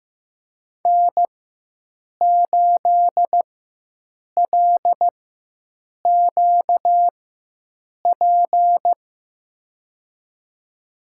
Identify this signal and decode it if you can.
Morse code, heard as a keyed tone, about 15 wpm, spaced as for 7 wpm, 700 Hz. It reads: N8LQP
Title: USA Call Signs - 5 Characters - 4x Character Spacing - 15wpm